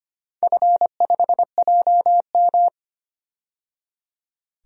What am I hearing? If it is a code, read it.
Morse code, heard as a keyed tone, about 25 wpm, 700 Hz. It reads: F5JM
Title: DXCC-100 Call Signs - 4 Characters - 25wpm